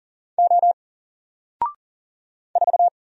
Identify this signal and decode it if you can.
Morse code, heard as a keyed tone, about 38 wpm, 700 Hz. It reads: O 4
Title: Single Letter-Number Alt 40wpm